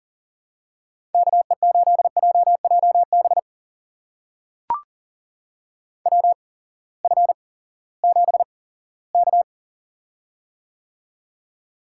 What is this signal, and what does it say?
Morse code, heard as a keyed tone, about 40 wpm, spaced as for 20 wpm, 700 Hz. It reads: KE8JJB WF7K
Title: USA Call Signs - 8x Character Spacing - 40wpm